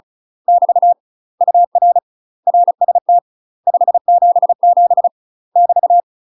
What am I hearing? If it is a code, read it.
Morse code, heard as a keyed tone, about 34 wpm, 700 Hz. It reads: <BT> UR RST 577 <BT>
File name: UK MASTER QSOs TRAINING / 35WPM / UK MASTER QSOs 0004